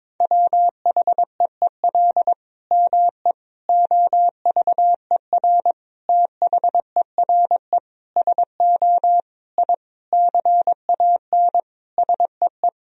Morse code, heard as a keyed tone, about 22 wpm, 700 Hz. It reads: WHEEL ME OVER THERE SO I CAN SEE
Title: Sentences from Top 1000 Words - Review Part 3 - 22wpm